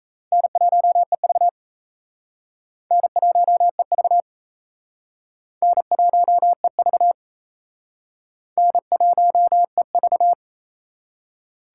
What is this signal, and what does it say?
Morse code, heard as a keyed tone, about 37 wpm, 700 Hz. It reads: N1EV N1EV N1EV N1EV